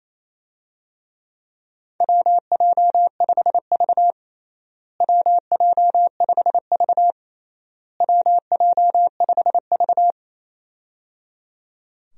Morse code, heard as a keyed tone, about 28 wpm, 700 Hz. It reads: WJ5V WJ5V WJ5V
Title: USA Call Signs - 4 Characters - Repeated 3x - 28wpm